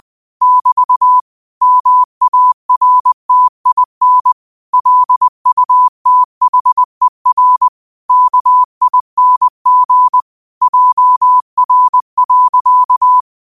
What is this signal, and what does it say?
Morse code, heard as a keyed tone, about 20 wpm, 1000 Hz. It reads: <BT> MARTIN LUTHER KING JR.
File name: qotdfiles / 20wpm